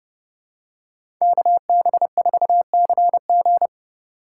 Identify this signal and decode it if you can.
Morse code, heard as a keyed tone, about 32 wpm, 700 Hz. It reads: KB4CG
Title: K1USN SST QSO Practice - 30wpm